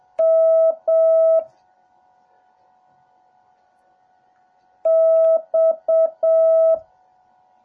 Morse code, heard as a keyed tone, about 7 wpm, 650 Hz. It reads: M X
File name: jh8cbh / 2021cw / cw08a